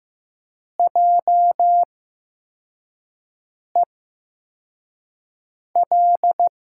Morse code, heard as a keyed tone, about 15 wpm, spaced as for 4 wpm, 700 Hz. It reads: JEL
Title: USA Call Signs - 6 Characters - 8x Character Spacing - 15wpm